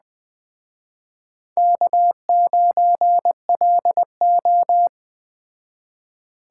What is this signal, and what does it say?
Morse code, heard as a keyed tone, about 21 wpm, 700 Hz. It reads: K9LO
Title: Speed Racing - USA 1x2 Call Signs 17wpm